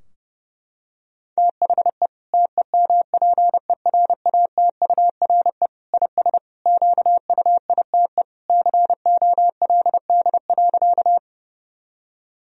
Morse code, heard as a keyed tone, about 30 wpm, 700 Hz. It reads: THE TEMPERATURE IS QUITE COLD.
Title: Sentences from Top 1000 Words - Review Part 4 - 30wpm